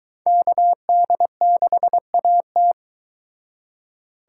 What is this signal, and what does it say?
Morse code, heard as a keyed tone, about 22 wpm, 700 Hz. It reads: KD6AT